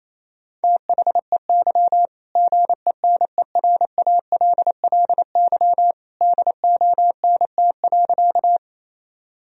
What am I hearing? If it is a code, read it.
Morse code, heard as a keyed tone, about 28 wpm, 700 Hz. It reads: THEY GENERALLY DONT.
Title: Sentences from Top 1000 Words - Review Part 4 - 28wpm